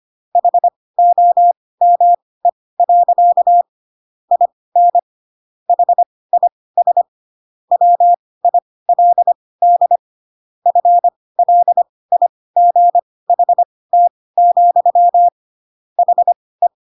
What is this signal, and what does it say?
Morse code, heard as a keyed tone, about 25 wpm, spaced as for 18 wpm, 700 Hz. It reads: HOME. IN HIS WILD FLIGHT, HE
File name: Pinocchio / Pinocchio ch4 12 25 ik1hge